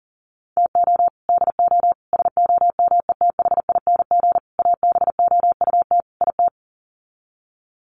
Morse code, heard as a keyed tone, about 40 wpm, 700 Hz. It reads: TO DO SOMETHING ABOUT IT
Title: Sets of 5 Words 40wpm